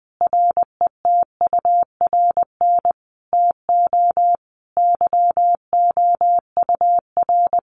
Morse code, heard as a keyed tone, about 20 wpm, 700 Hz. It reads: RETURN TO YOUR
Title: Sentences from Top 800 Words 20wpm